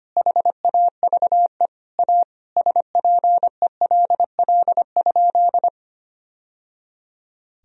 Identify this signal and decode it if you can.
Morse code, heard as a keyed tone, about 25 wpm, 700 Hz. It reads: HAVE A SPELL?
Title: Sentences from Top 1000 Words - Review Part 1 - 25wpm